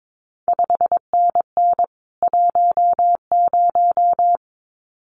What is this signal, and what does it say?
Morse code, heard as a keyed tone, about 22 wpm, 700 Hz. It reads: 5NN 10
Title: CQ WW DX Contest Exchange Encore - 22wpm